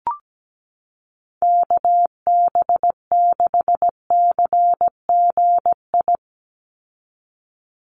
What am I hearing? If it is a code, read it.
Morse code, heard as a keyed tone, about 17 wpm, 700 Hz. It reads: KB6CGI
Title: USA Call Signs 17wpm